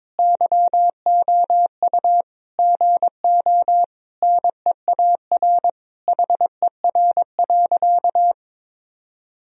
Morse code, heard as a keyed tone, about 21 wpm, 700 Hz. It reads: YOU GO NEAR HER.
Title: Sentences from Top 1000 Words - Review Part 4 - 22wpm